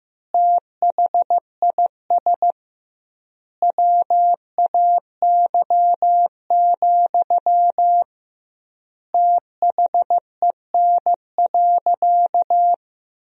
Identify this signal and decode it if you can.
Morse code, heard as a keyed tone, about 15 wpm, 700 Hz. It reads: THIS WAY, THEN.
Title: 2x Word-Spacing - Sentences from Top 400 Words - 15wpm